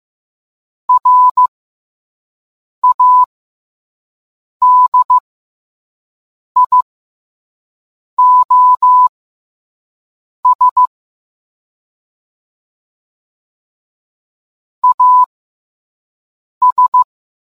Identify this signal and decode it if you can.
Morse code, heard as a keyed tone, about 15 wpm, spaced as for 5 wpm, 1000 Hz. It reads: RADIOS AS